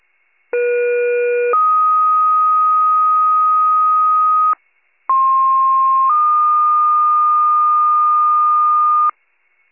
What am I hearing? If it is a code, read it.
Two-tone page, heard as a one-second tone, then a three-second tone, about 480 then 1250 Hz, about 1060 then 1220 Hz. It is heard from 0.5 to 9.1 s.